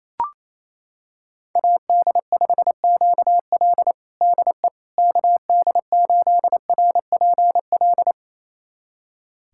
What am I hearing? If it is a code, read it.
Morse code, heard as a keyed tone, about 28 wpm, 700 Hz. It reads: AD5QL DE KD8RPL
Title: USA Callsign DE Callsign 28wpm